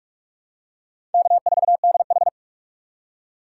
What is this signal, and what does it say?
Morse code, heard as a keyed tone, about 46 wpm, 700 Hz. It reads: K4DH
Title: USA Call Signs - 4 Characters - 2x Character Spacing - 45wpm